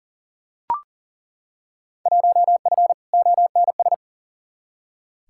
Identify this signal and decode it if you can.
Morse code, heard as a keyed tone, about 38 wpm, 700 Hz. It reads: 1F ONS
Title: Field Day Exchanges Encore - 40wpm